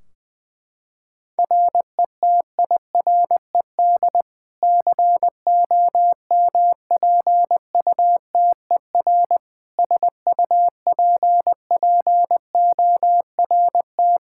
Morse code, heard as a keyed tone, about 20 wpm, 700 Hz. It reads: RETIRED COMPUTER SUPPORT
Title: Practice QSO - Exchange 3 - More Info - 20wpm